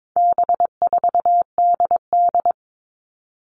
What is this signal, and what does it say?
Morse code, heard as a keyed tone, about 22 wpm, 700 Hz. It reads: B4DD
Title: All Cut Numbers - 4-Digit 22wpm